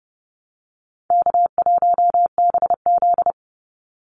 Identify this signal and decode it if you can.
Morse code, heard as a keyed tone, about 30 wpm, 700 Hz. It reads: K1BZ
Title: Call Signs Repeated 2x - 30wpm